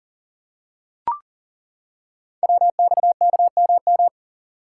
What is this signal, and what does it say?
Morse code, heard as a keyed tone, about 38 wpm, 690 Hz. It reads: WXKMM
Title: Code Groups - 5 Letters - 40wpm